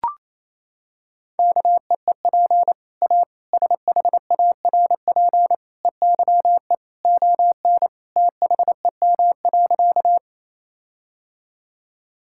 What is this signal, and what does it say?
Morse code, heard as a keyed tone, about 27 wpm, 700 Hz. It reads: KEEP A SHARP EYE ON THEM.
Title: Sentences from Top 1600 Words 28wpm